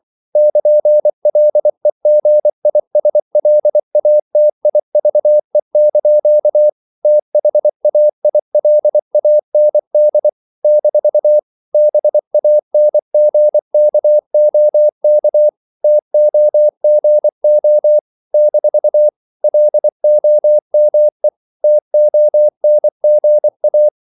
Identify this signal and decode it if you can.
Morse code, heard as a keyed tone, about 24 wpm, 600 Hz. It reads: <KN>LEGISLATIVE) THAILAND - BANGKOK TOGO - LOME TONGA
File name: CW / Practice / NationalCapitals24